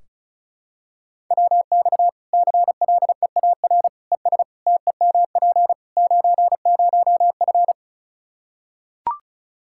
Morse code, heard as a keyed tone, about 34 wpm, 700 Hz. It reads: WX CLEAR ES TEMP 90F
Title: Practice QSO - Exchange 2 - Info - 35wpm